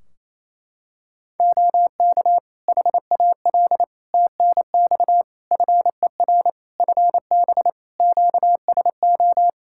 Morse code, heard as a keyed tone, about 28 wpm, 700 Hz. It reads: OK HAL TNX FER FB QSO
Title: Practice QSO - Exchange 4 - Ending - 28wpm